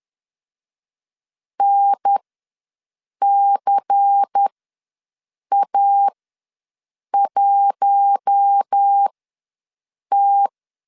Morse code, heard as a keyed tone, about 10 wpm, spaced as for 6 wpm, 800 Hz. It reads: NCA1T
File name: LEZIONI / 5 LBVC / 5 LBVC ws 10wpm 3733